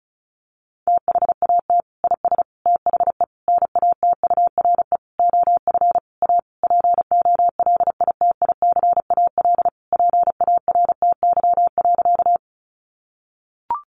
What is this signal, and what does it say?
Morse code, heard as a keyed tone, about 34 wpm, 700 Hz. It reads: THAT IS THE NATURE OF A POLITICAL PARTY.
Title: Sentences from Top 1400 Words 35wpm